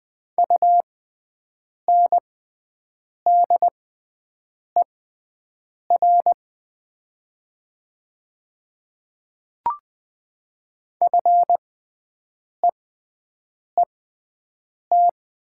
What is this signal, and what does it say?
Morse code, heard as a keyed tone, about 20 wpm, spaced as for 7 wpm, 690 Hz. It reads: UNDER FEET
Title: 6x Character Spacing - Top 300 Words - 20wpm